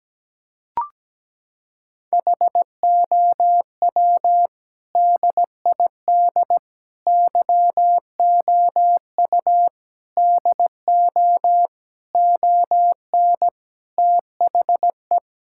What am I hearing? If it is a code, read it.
Morse code, heard as a keyed tone, about 17 wpm, 700 Hz. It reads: HOW DID YOU DO ON THE
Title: Sentences from Top 1000 Words - Review Part 5 - 17wpm